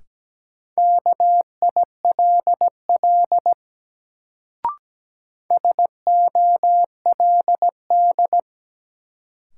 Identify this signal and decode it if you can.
Morse code, heard as a keyed tone, about 17 wpm, 700 Hz. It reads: KILL SOLD
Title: Sentences from Top 1400 Words - New Word Drill - 17wpm